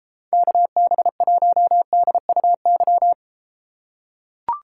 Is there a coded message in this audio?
Yes: Morse code from 0.3 to 3.1 s, heard as a keyed tone, about 33 wpm, 700 Hz. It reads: KB1DUY